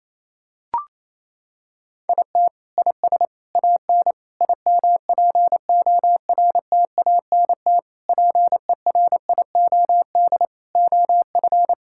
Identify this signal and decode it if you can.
Morse code, heard as a keyed tone, about 28 wpm, 700 Hz. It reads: IT IS AN IMPORTANT PERIOD OF
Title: Sentences from Top 1000 Words - Review Part 5 - 28wpm